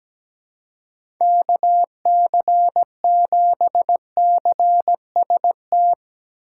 Morse code, heard as a keyed tone, about 17 wpm, 700 Hz. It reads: KC7CST
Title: USA Call Signs - 6 Characters - 6x Character Spacing - 17wpm